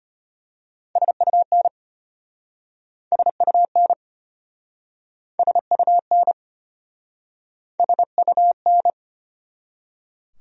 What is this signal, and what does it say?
Morse code, heard as a keyed tone, about 33 wpm, 700 Hz. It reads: SUN SUN SUN SUN